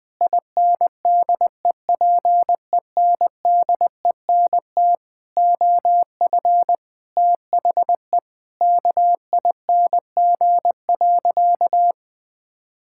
Morse code, heard as a keyed tone, about 20 wpm, 700 Hz. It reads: INDEPENDENT OF THE KING.
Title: Sentences from Top 1000 Words - Review Part 6 - 20wpm